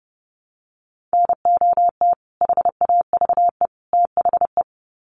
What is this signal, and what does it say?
Morse code, heard as a keyed tone, about 30 wpm, 700 Hz. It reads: NOT HAVE THE
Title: Sets of 3 Words 30wpm